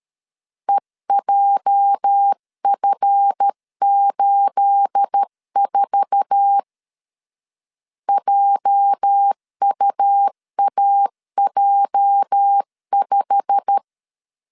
Morse code, heard as a keyed tone, about 13 wpm, 800 Hz. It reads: EJF84 JUAJ5
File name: LEZIONI / 9 28 12wpm 2351